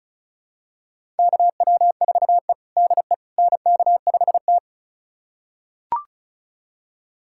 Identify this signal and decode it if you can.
Morse code, heard as a keyed tone, about 36 wpm, 700 Hz. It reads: KW4E DE NK5T
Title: USA Callsign DE Callsign - 4 Characters - 35wpm